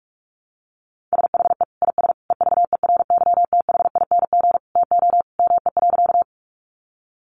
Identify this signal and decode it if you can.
Morse code, heard as a keyed tone, about 44 wpm, 700 Hz. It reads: SHE IS EVERYTHING TO ME.
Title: Sentences from Top 700 Words 45wpm